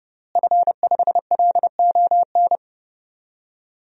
Morse code, heard as a keyed tone, about 32 wpm, 700 Hz. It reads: F5LON
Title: DXCC-100 Call Signs - 5 Characters - Repeated 2x - 30wpm